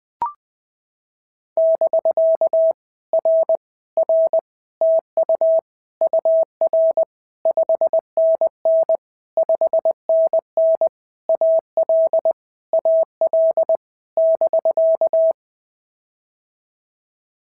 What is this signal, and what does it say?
Morse code, heard as a keyed tone, about 21 wpm, 650 Hz. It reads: <BK> R R TU UR 5NN 5NN AL AL <BK>
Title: POTA QSO Practice - 20wpm